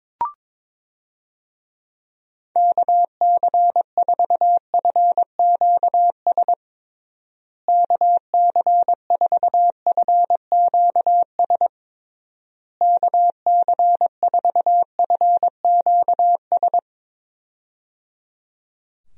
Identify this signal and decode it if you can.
Morse code, heard as a keyed tone, about 22 wpm, 700 Hz. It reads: KC4FQS KC4FQS KC4FQS
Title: USA Call Signs - 6 Characters - Repeated 3x - 22wpm